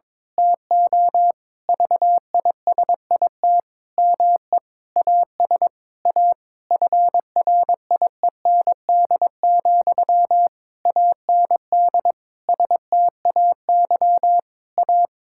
Morse code, heard as a keyed tone, about 21 wpm, 700 Hz. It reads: TO VISIT ME AS A FRIEND, AND STAY A
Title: Sentences from Top 1000 Words - Review Part 6 - 22wpm